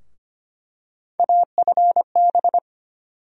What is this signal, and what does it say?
Morse code, heard as a keyed tone, about 25 wpm, 700 Hz. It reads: AFB
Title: USA Call Sign Postfixes - 3 Letters 25wpm